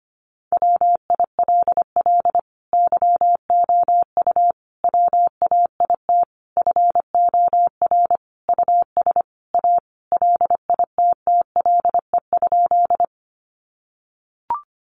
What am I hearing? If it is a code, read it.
Morse code, heard as a keyed tone, about 25 wpm, 700 Hz. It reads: WILL YOU WAIT FOR US A LITTLE?
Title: Sentences from Top 900 Words 25wpm